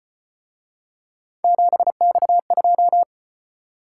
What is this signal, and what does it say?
Morse code, heard as a keyed tone, about 33 wpm, 700 Hz. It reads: 7X2